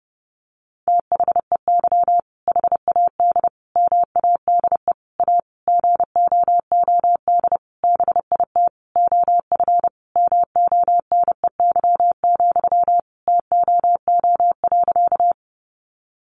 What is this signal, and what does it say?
Morse code, heard as a keyed tone, about 30 wpm, 700 Hz. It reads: THEY HAD MADE A GOOD BIT OF MONEY, TOO.
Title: Sentences from Top 900 Words 30wpm